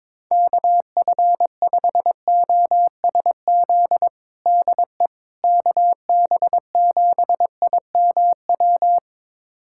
Morse code, heard as a keyed tone, about 22 wpm, 700 Hz. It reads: KF5OSZ DE KB7IMW
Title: USA Callsign DE Callsign 22wpm